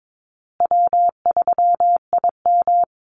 Morse code, heard as a keyed tone, about 22 wpm, 700 Hz. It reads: W3IM
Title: USA Call Signs Repeated 2x 22wpm